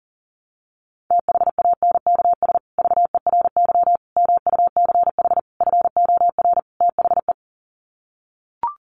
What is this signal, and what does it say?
Morse code, heard as a keyed tone, about 40 wpm, 700 Hz. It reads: THANKS VERY MUCH FOR THE
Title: Sets of 5 Words Encore2 40wpm